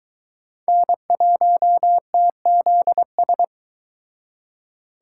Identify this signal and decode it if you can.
Morse code, heard as a keyed tone, about 23 wpm, 700 Hz. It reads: N1TZS